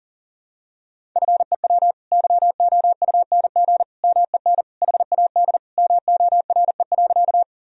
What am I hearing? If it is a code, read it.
Morse code, heard as a keyed tone, about 40 wpm, 700 Hz. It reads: FEW YOUNG MEN HAD MORE.
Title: Sentences from Top 1000 Words - Review Part 1 - 40wpm